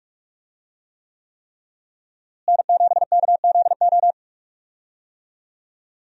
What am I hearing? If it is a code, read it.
Morse code, heard as a keyed tone, about 47 wpm, 700 Hz. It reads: N7KZO
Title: USA Call Signs - 5 Characters - Repeated 3x - 45wpm